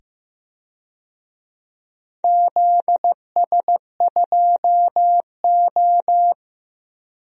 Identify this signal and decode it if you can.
Morse code, heard as a keyed tone, about 15 wpm, 700 Hz. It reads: ZS2O
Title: DXCC-100 Call Signs - 4 Characters - 15wpm